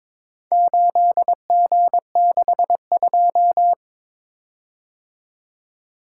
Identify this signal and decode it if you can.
Morse code, heard as a keyed tone, about 21 wpm, 700 Hz. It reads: 8G62
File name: practice / 22wpm / Four Letter-Number Combinations - 22wpm